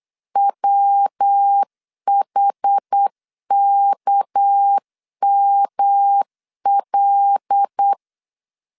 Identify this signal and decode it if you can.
Morse code, heard as a keyed tone, about 8 wpm, 800 Hz. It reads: WHKML
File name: LEZIONI / 6 PKX0 / 6 PK 8wpm 3003